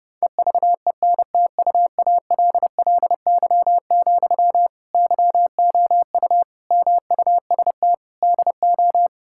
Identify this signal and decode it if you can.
Morse code, heard as a keyed tone, about 30 wpm, 700 Hz. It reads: EVENTUALLY, YOU MUST DO
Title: Sentences from Top 1000 Words - Review Part 6 - 30wpm